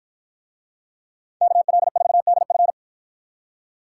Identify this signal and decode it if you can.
Morse code, heard as a keyed tone, about 50 wpm, 700 Hz. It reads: KL4DF